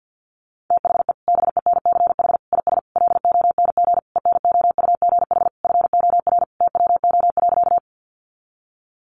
Morse code, heard as a keyed tone, about 50 wpm, 700 Hz. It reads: THE BENCH IS LONG ENOUGH FOR TWO.
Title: Sentences from Top 1400 Words 50wpm